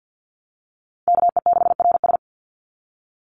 Morse code, heard as a keyed tone, about 48 wpm, 700 Hz. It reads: KE6RS
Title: NA SOTA Chaser Callsigns 50wpm